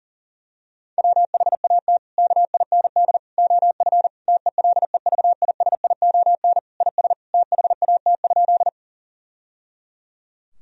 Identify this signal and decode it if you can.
Morse code, heard as a keyed tone, about 40 wpm, 700 Hz. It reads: WHAT KIND OF TELEVISION IS THAT?
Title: Sentences from Top 1000 Words - Review Part 4 - 40wpm